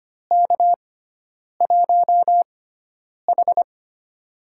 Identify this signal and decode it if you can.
Morse code, heard as a keyed tone, about 25 wpm, spaced as for 9 wpm, 700 Hz. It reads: K1H
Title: USA Call Signs - 4 Characters - 6x Character Spacing - 25wpm